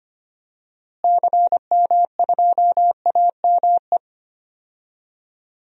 Morse code, heard as a keyed tone, about 25 wpm, 700 Hz. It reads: CM2AME
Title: DXCC-100 Call Signs - 6 Characters - Repeated 2x - 25wpm